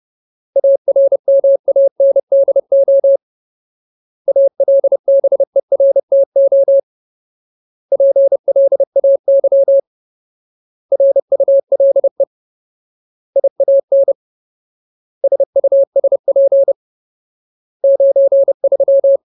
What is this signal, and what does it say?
Morse code, heard as a keyed tone, about 30 wpm, 550 Hz. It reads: ARMANDO ALBERTO PLAY RULE IAN SUSP 93
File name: Ultimate Word List / 30WPM Ultimate